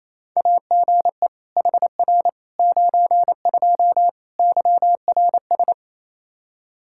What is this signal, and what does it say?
Morse code, heard as a keyed tone, about 28 wpm, 700 Hz. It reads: AGE HR 92 YRS
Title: Practice QSO - Exchange 3 - More Info - 28wpm